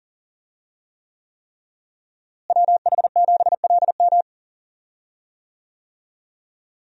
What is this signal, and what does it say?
Morse code, heard as a keyed tone, about 40 wpm, 700 Hz. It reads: WH7LM
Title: USA Call Signs - 5 Characters - Repeated 3x - 40wpm